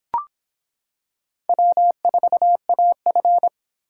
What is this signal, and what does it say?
Morse code, heard as a keyed tone, about 25 wpm, 700 Hz. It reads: W4AF